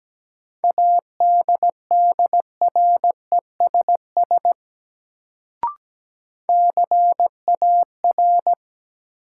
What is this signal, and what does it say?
Morse code, heard as a keyed tone, about 17 wpm, 700 Hz. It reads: ADDRESS CAR
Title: Sets of 5 Words Encore - New Word Drill - 17wpm